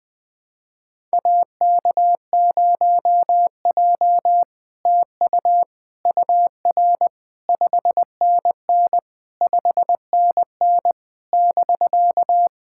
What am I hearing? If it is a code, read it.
Morse code, heard as a keyed tone, about 20 wpm, 700 Hz. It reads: AK0J TU UR 5NN 5NN <BK>
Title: SOTA QSO Practice 20wpm